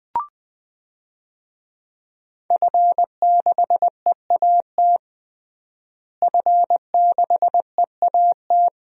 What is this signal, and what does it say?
Morse code, heard as a keyed tone, about 20 wpm, 700 Hz. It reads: F6EAT F6EAT
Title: DXCC-100 Call Signs - 5 Characters - Repeated 3x - 20wpm